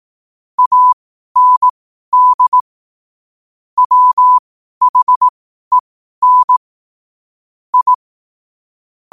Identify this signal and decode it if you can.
Morse code, heard as a keyed tone, about 18 wpm, spaced as for 12 wpm, 1000 Hz. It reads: AND WHEN I
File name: qotdfiles / 12wpm